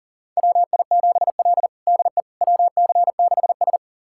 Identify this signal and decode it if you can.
Morse code, heard as a keyed tone, about 40 wpm, 700 Hz. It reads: WI7L DE WC6S
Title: USA Callsign DE Callsign - 4 Characters - 40wpm